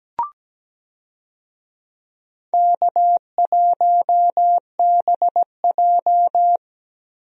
Morse code, heard as a keyed tone, about 17 wpm, 700 Hz. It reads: K1BJ
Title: USA Call Signs - 4 Characters - 17wpm